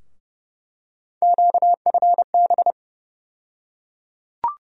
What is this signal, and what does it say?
Morse code, heard as a keyed tone, about 32 wpm, 700 Hz. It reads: QFB
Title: USA Call Sign Postfixes - 3 Letters 30wpm